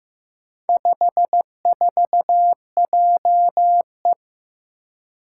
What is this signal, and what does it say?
Morse code, heard as a keyed tone, about 15 wpm, 700 Hz. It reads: 54JE
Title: Four Challenging Letter-Number Combinations 15wpm